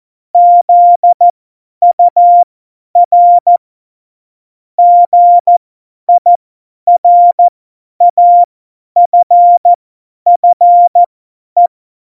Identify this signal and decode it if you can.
Morse code, heard as a keyed tone, about 14 wpm, spaced as for 10 wpm, 700 Hz. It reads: ZUR GIRAFFE